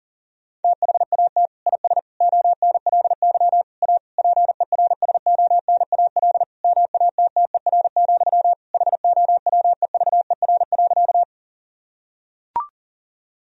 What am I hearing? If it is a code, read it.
Morse code, heard as a keyed tone, about 40 wpm, 700 Hz. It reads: THAT IS ONLY A PERSONAL MATTER, HOWEVER.
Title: Sentences from Top 1000 Words - Review Part 4 - 40wpm